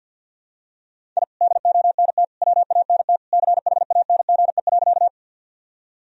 Morse code, heard as a keyed tone, about 52 wpm, 700 Hz. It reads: I DONT WANT CHANGE.
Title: Sentences from Top 1000 Words - Review Part 1 - 50wpm